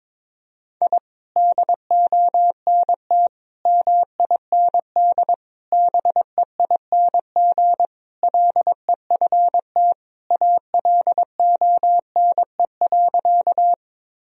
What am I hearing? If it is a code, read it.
Morse code, heard as a keyed tone, about 22 wpm, 700 Hz. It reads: I DONT MIND BEING LEFT ALONE.
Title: Sentences from Top 1000 Words - Review Part 3 - 22wpm